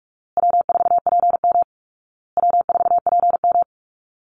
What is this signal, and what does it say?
Morse code, heard as a keyed tone, about 47 wpm, 700 Hz. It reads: W4PM W4PM
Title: Easy Call Signs Repeated 3x - 45wpm